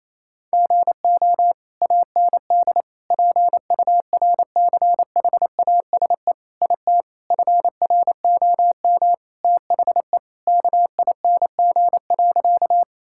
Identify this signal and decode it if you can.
Morse code, heard as a keyed tone, about 28 wpm, 700 Hz. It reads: GO AND PURCHASE IT FROM THE KING.
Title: Sentences from Top 1000 Words - Review Part 5 - 28wpm